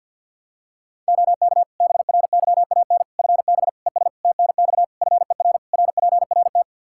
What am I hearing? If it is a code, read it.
Morse code, heard as a keyed tone, about 53 wpm, 700 Hz. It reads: OK BRYAN FB ES TNX FER RPRT
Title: Practice QSO - Exchange 2 - Info - 50wpm